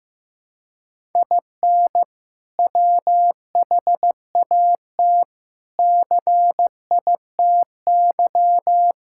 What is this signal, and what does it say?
Morse code, heard as a keyed tone, about 16 wpm, 700 Hz. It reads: IN WHAT CITY
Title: Sentences from Top 1000 Words - Review Part 6 - 15wpm